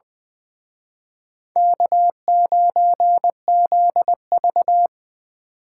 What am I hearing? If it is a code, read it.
Morse code, heard as a keyed tone, about 24 wpm, 700 Hz. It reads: K9ZV
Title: Speed Racing - USA 1x2 Call Signs 20wpm